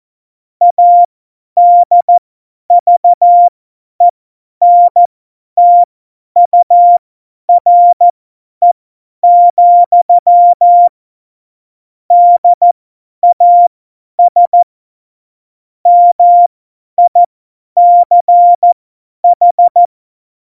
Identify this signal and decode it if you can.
Morse code, heard as a keyed tone, about 14 wpm, spaced as for 10 wpm, 700 Hz. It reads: ADVENTURE, DAS MICH
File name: Morsekurs / Witze / Witz 29. Januar 2019, 50 F70